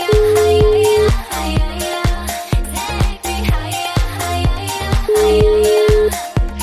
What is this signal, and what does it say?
Ringback tone (Europe, 425 Hz), with noise.